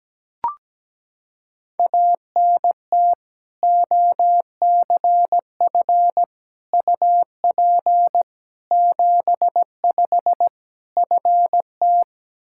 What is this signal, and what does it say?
Morse code, heard as a keyed tone, about 17 wpm, 700 Hz. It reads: ANT OCF UP 75 FT
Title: Practice QSO - Exchange 2 - Info - 17wpm